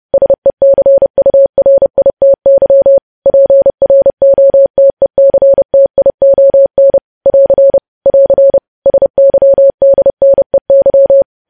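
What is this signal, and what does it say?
Morse code, heard as a keyed tone, about 30 wpm, 550 Hz. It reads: SECURITY PROTECTION <AR> <AR> SYDNEY
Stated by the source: News headlines sent in Morse code at 30.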